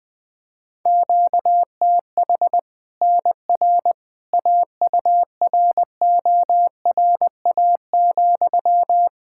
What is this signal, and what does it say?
Morse code, heard as a keyed tone, about 20 wpm, 700 Hz. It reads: QTH NR AURORA,
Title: Practice QSO - Exchange 1 - Beginning - 20wpm